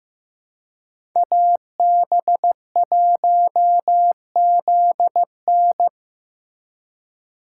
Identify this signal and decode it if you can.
Morse code, heard as a keyed tone, about 18 wpm, 700 Hz. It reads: AB1ZN
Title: Speed Racing - USA 2x2 Call Signs 15wpm